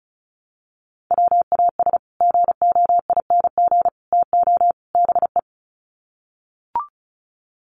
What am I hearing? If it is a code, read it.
Morse code, heard as a keyed tone, about 34 wpm, 700 Hz. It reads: WAS GOING TO BE
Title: Sets of 4 Words 35wpm